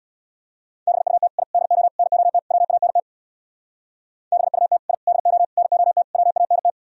Morse code, heard as a keyed tone, about 64 wpm, 700 Hz. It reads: BUT I DO NOT WANT BUT I DO NOT WANT